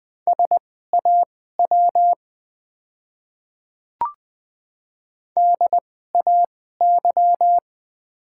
Morse code, heard as a keyed tone, about 21 wpm, spaced as for 15 wpm, 700 Hz. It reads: SAW DAY
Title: 2x Character Spacing - Top 500 Words - 20wpm